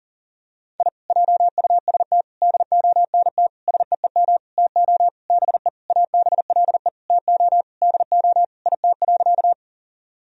Morse code, heard as a keyed tone, about 40 wpm, 700 Hz. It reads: I JUST DONT SEEM TO BE ABLE TO DO IT.
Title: Sentences from Top 1000 Words - Review Part 2 - 40wpm